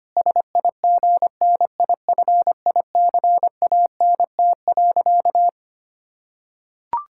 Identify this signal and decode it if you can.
Morse code, heard as a keyed tone, about 25 wpm, 700 Hz. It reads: SIGNIFICANT.
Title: Sentences from Top 1000 Words - Review Part 6 - 25wpm